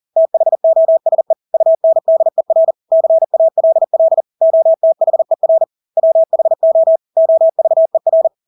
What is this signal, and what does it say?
Morse code, heard as a keyed tone, about 40 wpm, 650 Hz. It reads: THOSE UNDER CALL OTHER WHO OVER
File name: Engelske ord 40-40 wpm